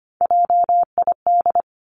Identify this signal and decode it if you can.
Morse code, heard as a keyed tone, about 25 wpm, 700 Hz. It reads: JID
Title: Three Challenging Letter-Number Combinations 25wpm